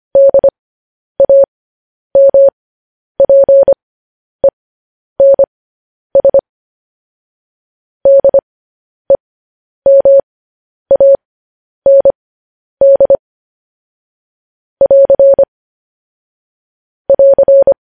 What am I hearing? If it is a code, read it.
Morse code, heard as a keyed tone, about 25 wpm, spaced as for 10 wpm, 550 Hz. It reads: DAMPENS DEMAND <AR> <AR>